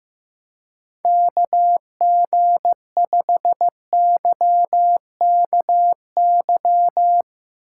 Morse code, heard as a keyed tone, about 15 wpm, 700 Hz. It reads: KG5YKY
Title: Speed Racing - USA 2x3 Call Signs 15wpm